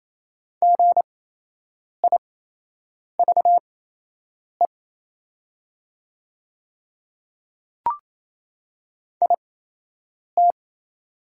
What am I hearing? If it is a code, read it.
Morse code, heard as a keyed tone, about 28 wpm, spaced as for 8 wpm, 690 Hz. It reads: GIVE IT
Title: 8x Character Spacing - Top 200 Words - 28wpm